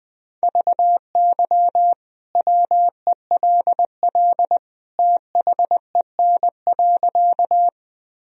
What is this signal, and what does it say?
Morse code, heard as a keyed tone, about 20 wpm, 700 Hz. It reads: VY WELL THEN.
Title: Sentences from Top 200 Words and Abbreviations 20wpm